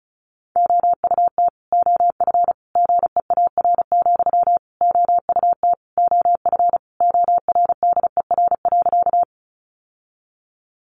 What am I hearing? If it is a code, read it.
Morse code, heard as a keyed tone, about 34 wpm, 700 Hz. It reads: OUT OF GEAR, OUT OF ORDER.
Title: Sentences from Top 1400 Words 35wpm